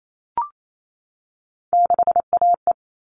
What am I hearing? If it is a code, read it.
Morse code, heard as a keyed tone, about 28 wpm, 700 Hz. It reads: 6AE
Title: All Cut Numbers - 3-Digit 28wpm